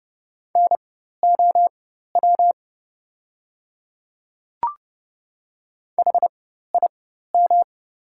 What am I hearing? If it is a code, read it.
Morse code, heard as a keyed tone, about 30 wpm, spaced as for 14 wpm, 700 Hz. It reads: NOW HIM
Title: 4x Character Spacing - Top 100 Words - 30wpm